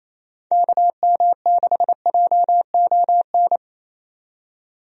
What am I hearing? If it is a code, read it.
Morse code, heard as a keyed tone, about 27 wpm, 700 Hz. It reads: KM6JON
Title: USA Call Signs - 6 Characters - 28wpm